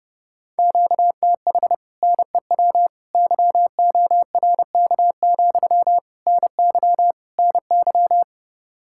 Morse code, heard as a keyed tone, about 30 wpm, 700 Hz. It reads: QTH NEW YORK, NY NY